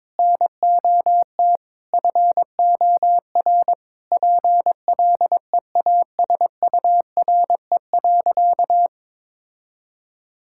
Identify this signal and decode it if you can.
Morse code, heard as a keyed tone, about 22 wpm, 700 Hz. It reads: NOT FOR PLEASURE.
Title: Sentences from Top 1600 Words 22wpm